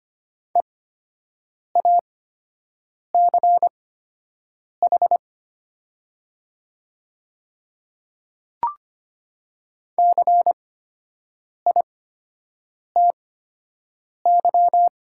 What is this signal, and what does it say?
Morse code, heard as a keyed tone, about 25 wpm, spaced as for 7 wpm, 700 Hz. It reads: EACH CITY
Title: 8x Character Spacing - Top 300 Words - 25wpm